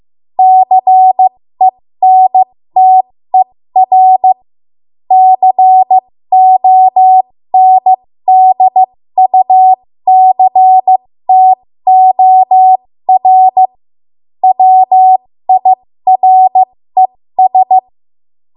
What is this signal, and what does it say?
Morse code, heard as a keyed tone, about 15 wpm, spaced as for 13 wpm, 750 Hz. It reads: CENTER CONDUCTOR WIRES
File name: Morse / 13WPM